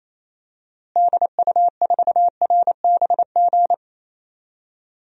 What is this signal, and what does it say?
Morse code, heard as a keyed tone, about 28 wpm, 700 Hz. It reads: DU4RBG
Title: DXCC-100 Call Signs - 6 Characters - Repeated 3x - 28wpm